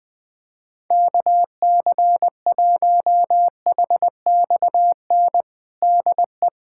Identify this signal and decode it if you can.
Morse code, heard as a keyed tone, about 20 wpm, 700 Hz. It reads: KC1HXN DE
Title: USA Callsign DE Callsign - 6 Characters - 20wpm